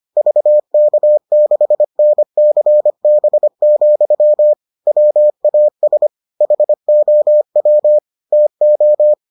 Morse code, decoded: VK6NCB, WAS HOW TO

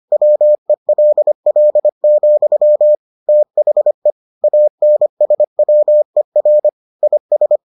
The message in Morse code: WELL, THE ANSWER IS

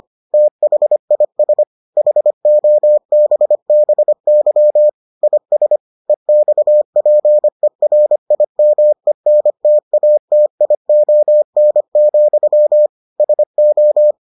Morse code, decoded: THIS HOBBY IS EXPERIMENTATION, SO

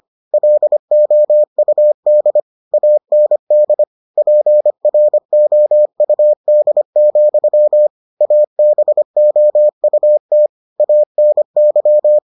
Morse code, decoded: LOUD AND PROUD, ABOUT ANY

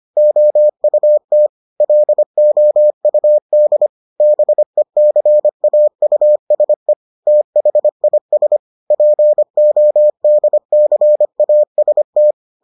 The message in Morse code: OUT LOUD BECAUSE THIS PODCAST

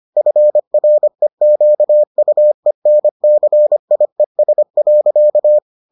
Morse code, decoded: FREQUENCIES.

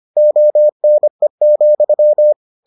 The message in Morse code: ONE,

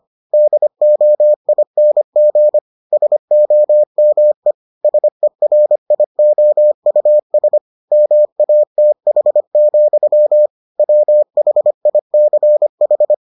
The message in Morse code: DOING SOME SERIOUS MATH, WHICH

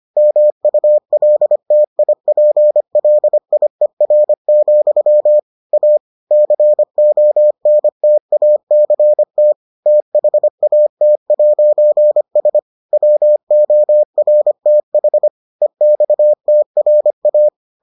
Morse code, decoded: MULTIPLIER, A CONTACT THAT'S WORTH EXTRA

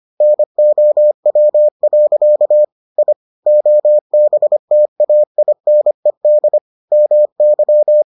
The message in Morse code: NOW. I OBTAINED MY

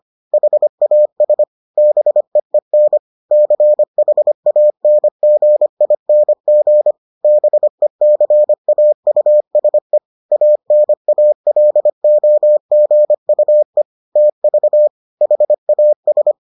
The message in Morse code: HAS BEEN CHANGING BECAUSE ANALOGUE TV HAS

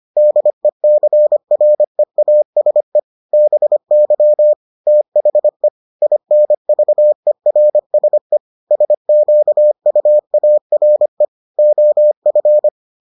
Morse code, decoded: DECREASE BY THE INVERSE SQUARE OF

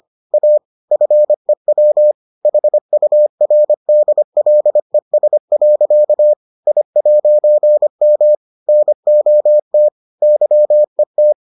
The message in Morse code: A FEW HURDLES. I'M NOT YET

